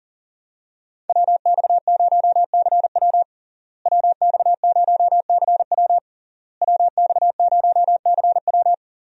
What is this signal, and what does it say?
Morse code, heard as a keyed tone, about 40 wpm, 700 Hz. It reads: WX0CW WX0CW WX0CW
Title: USA Call Signs - 5 Characters - Repeated 3x - 40wpm